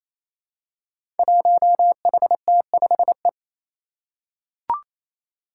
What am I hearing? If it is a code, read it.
Morse code, heard as a keyed tone, about 27 wpm, 700 Hz. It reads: 1HT5E
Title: Five Challenging Letter-Number Combinations 28wpm